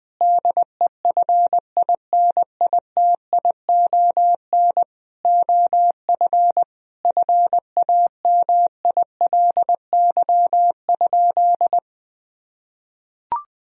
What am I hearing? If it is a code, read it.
Morse code, heard as a keyed tone, about 20 wpm, 700 Hz. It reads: DEFINITION OF FAMILY?
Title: Sentences from Top 1000 Words - Review Part 4 - 20wpm